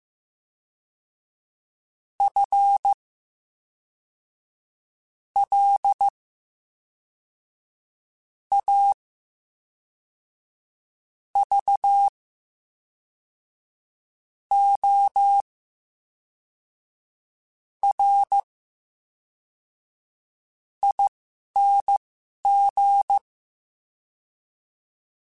Morse code, heard as a keyed tone, about 15 wpm, spaced as for 4 wpm, 800 Hz. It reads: FLAVORING
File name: Special INGs / Special INGs-mp3s / VerySlowToFaster Ing @10